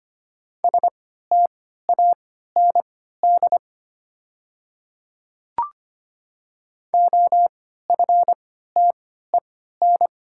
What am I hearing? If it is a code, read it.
Morse code, heard as a keyed tone, about 25 wpm, spaced as for 14 wpm, 700 Hz. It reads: STAND OFTEN
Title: 3x Character Spacing - Top 500 Words - 25wpm